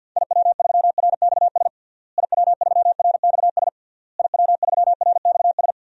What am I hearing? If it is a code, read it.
Morse code, heard as a keyed tone, about 51 wpm, 700 Hz. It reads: IW3RXS IW3RXS IW3RXS
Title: DXCC-100 Call Signs - 6 Characters - Repeated 3x - 50wpm